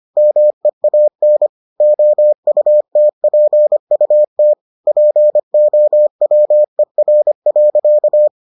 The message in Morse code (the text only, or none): MEAN OUTPUT POWER.